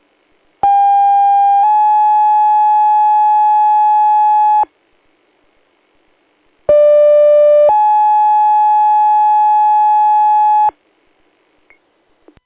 Two-tone page, heard as a one-second tone, then a three-second tone, about 800 then 850 Hz, about 590 then 850 Hz. It is heard from 0.6 to 10.7 s.